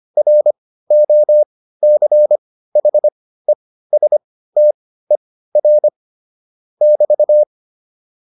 Morse code, heard as a keyed tone, about 25 wpm, spaced as for 15 wpm, 600 Hz. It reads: ROCHESTER <BT>
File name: QSO402 15